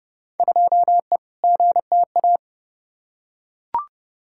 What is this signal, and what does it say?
Morse code, heard as a keyed tone, about 30 wpm, 700 Hz. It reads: 2E GTA
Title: Field Day Exchanges Encore - 30wpm